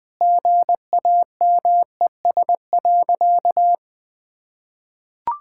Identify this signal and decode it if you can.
Morse code, heard as a keyed tone, about 20 wpm, 700 Hz. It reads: GAMES.
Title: Easy Sentences with Hints - 20wpm